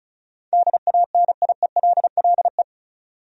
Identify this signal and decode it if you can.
Morse code, heard as a keyed tone, about 36 wpm, 700 Hz. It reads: DANIELLE
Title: USA Common Names - 35wpm